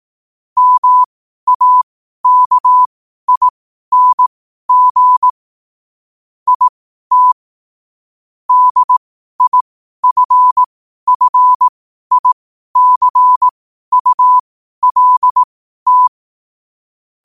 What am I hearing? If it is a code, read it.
Morse code, heard as a keyed tone, about 18 wpm, spaced as for 12 wpm, 1000 Hz. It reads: MAKING IT DIFFICULT
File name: qotdfiles / 12wpm